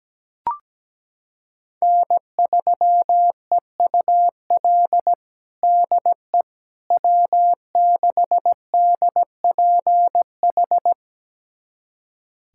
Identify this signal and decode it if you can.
Morse code, heard as a keyed tone, about 17 wpm, 700 Hz. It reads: N3EUL DE W6DPH
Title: USA Callsign DE Callsign - 5 Characters - 17wpm